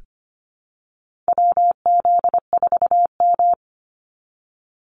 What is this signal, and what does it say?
Morse code, heard as a keyed tone, about 25 wpm, 690 Hz. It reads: WZ4M
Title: NA SOTA Activator Callsigns 25wpm